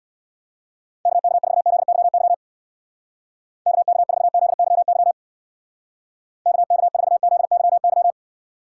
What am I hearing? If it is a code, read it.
Morse code, heard as a keyed tone, about 60 wpm, 700 Hz. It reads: KK4ZYY KK4ZYY KK4ZYY
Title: Speed Racing - USA 2x3 Call Signs 50wpm